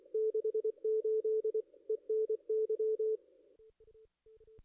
Morse code, heard as a keyed tone, about 25 wpm, 440 Hz, with radio noise. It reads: 68 ENY
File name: contests / ss-cw-k5tr-op-n5rz -2300